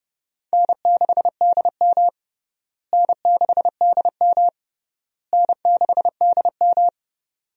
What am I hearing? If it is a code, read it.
Morse code, heard as a keyed tone, about 30 wpm, 700 Hz. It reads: N6DM N6DM N6DM
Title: USA Call Signs - 4 Characters - Repeated 3x - 30wpm